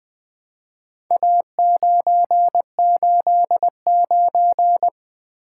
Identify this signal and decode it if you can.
Morse code, heard as a keyed tone, about 20 wpm, 700 Hz. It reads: A989
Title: Four Letter-Number Combinations - 20wpm